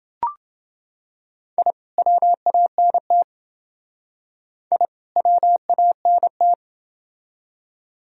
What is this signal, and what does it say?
Morse code, heard as a keyed tone, about 28 wpm, 700 Hz. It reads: I WANT I WANT